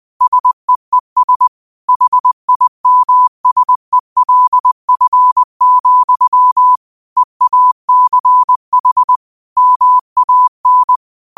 Morse code, heard as a keyed tone, about 20 wpm, 1000 Hz. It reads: SEES HIMSELF, EACH MAN
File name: qotdfiles / 20wpm